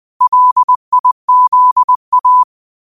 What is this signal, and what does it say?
Morse code, heard as a keyed tone, about 20 wpm, 1000 Hz. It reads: LIZA
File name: qotdfiles / 20wpm